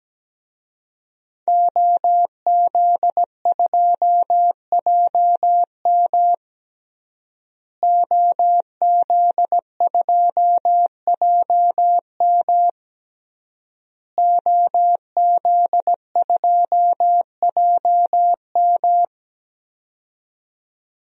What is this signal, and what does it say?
Morse code, heard as a keyed tone, about 17 wpm, 700 Hz. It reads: OZ2JM OZ2JM OZ2JM
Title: DXCC-100 Call Signs - 5 Characters - Repeated 3x - 17wpm